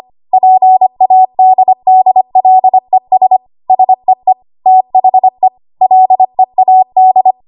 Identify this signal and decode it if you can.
Morse code, heard as a keyed tone, about 25 wpm, 750 Hz. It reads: PADDLES SEE THE LEAD
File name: Archive / 25 WPM / 25WPM